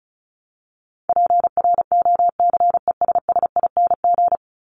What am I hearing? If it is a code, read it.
Morse code, heard as a keyed tone, about 34 wpm, 700 Hz. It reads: PROCESSING
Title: ING Suffix - Top 100 Words with Base Words 35wpm